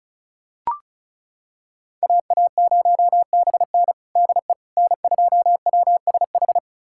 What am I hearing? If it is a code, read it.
Morse code, heard as a keyed tone, about 36 wpm, 700 Hz. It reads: AA0BN DE N2WSH
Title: USA Callsign DE Callsign - 5 Characters - 35wpm